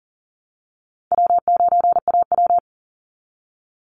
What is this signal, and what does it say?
Morse code, heard as a keyed tone, about 40 wpm, 700 Hz. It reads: W9AW
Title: Call Signs - 40wpm